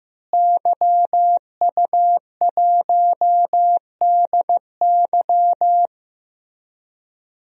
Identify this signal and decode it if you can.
Morse code, heard as a keyed tone, about 15 wpm, 700 Hz. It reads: YU1DY
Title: DXCC-100 Call Signs - 5 Characters - Repeated 3x - 15wpm